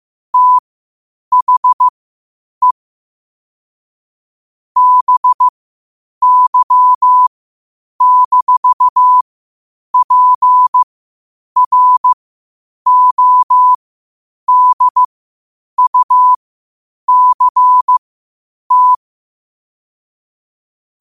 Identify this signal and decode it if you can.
Morse code, heard as a keyed tone, about 15 wpm, spaced as for 8 wpm, 1000 Hz. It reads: THE BY-PRODUCT
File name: qotdfiles / 8wpm